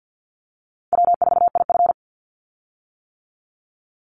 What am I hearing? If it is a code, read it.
Morse code, heard as a keyed tone, about 50 wpm, 700 Hz. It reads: W4IF
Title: Easy Call Signs - 50wpm